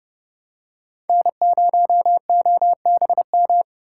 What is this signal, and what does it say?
Morse code, heard as a keyed tone, about 30 wpm, 700 Hz. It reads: N0OBM